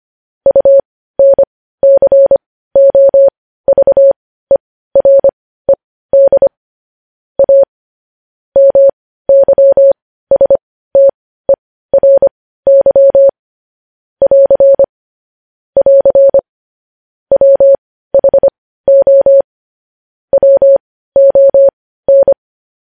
Morse code, heard as a keyed tone, about 25 wpm, spaced as for 15 wpm, 550 Hz. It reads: UNCOVERED A MYSTERY <AR> <AR> WHO WON
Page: News headlines at 15